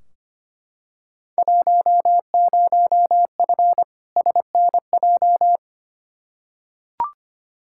Morse code, heard as a keyed tone, about 25 wpm, 700 Hz. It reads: 10F SNJ
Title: Field Day Exchanges Encore - 25wpm